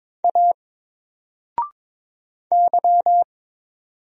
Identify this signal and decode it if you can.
Morse code, heard as a keyed tone, about 22 wpm, 700 Hz. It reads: A Y
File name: Single-Letter-Number-22wpm